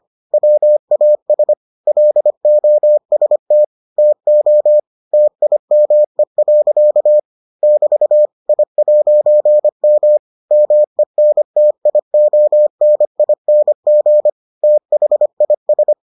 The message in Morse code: WAS LOST TO TIME. <BT> I'M MENTIONING THIS

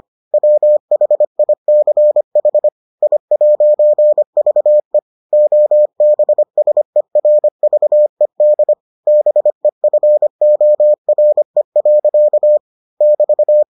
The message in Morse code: WHICH I'VE OBSERVED BEFORE. <BT>

